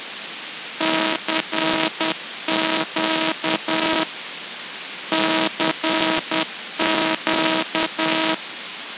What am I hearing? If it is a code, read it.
Morse code, heard as a keyed tone, about 10 wpm, 320 Hz, with noise. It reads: CQ CQ
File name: ve7cnf / SparkGap / SparkRX NoBlower